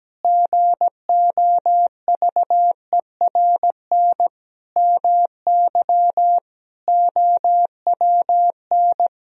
Morse code, heard as a keyed tone, about 17 wpm, 700 Hz. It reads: GOVERN MY OWN
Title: Sentences from Top 1000 Words - Review Part 2 - 17wpm